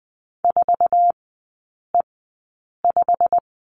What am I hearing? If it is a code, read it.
Morse code, heard as a keyed tone, about 20 wpm, spaced as for 8 wpm, 700 Hz. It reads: 4E5
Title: 087 Lesson 15e - Number 5 - Three Characters 20wpm